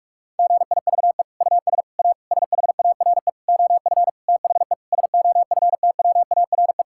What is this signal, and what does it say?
Morse code, heard as a keyed tone, about 41 wpm, 700 Hz. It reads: GIVE US A SHARE OF THE SOFTWARE
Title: Sentences from Top 1000 Words - Review Part 6 - 45wpm